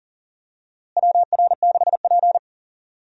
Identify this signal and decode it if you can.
Morse code, heard as a keyed tone, about 40 wpm, 700 Hz. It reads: WR6P
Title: USA Call Signs - 4 Characters - Repeated 2x - 40wpm